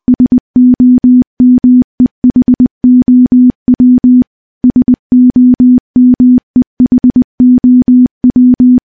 Morse code, heard as a keyed tone, about 20 wpm, 250 Hz. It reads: SOMEHOW SOMEHOW
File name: morse-training-20WPM